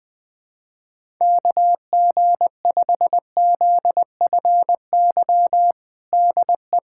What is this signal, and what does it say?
Morse code, heard as a keyed tone, about 20 wpm, 700 Hz. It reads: KG5ZFY DE
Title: USA Callsign DE Callsign - 6 Characters - 20wpm